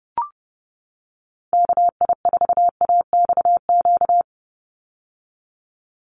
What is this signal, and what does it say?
Morse code, heard as a keyed tone, about 30 wpm, 700 Hz. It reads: KI4AXQ
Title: USA Call Signs 30wpm